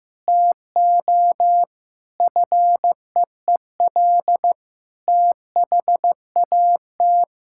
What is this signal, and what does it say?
Morse code, heard as a keyed tone, about 15 wpm, 700 Hz. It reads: TO FEEL THAT
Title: Sentences from Top 1000 Words - Review Part 5 - 15wpm